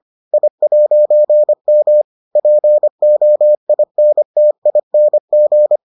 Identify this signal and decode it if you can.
Morse code, heard as a keyed tone, about 25 wpm, 600 Hz. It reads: I'M POINTING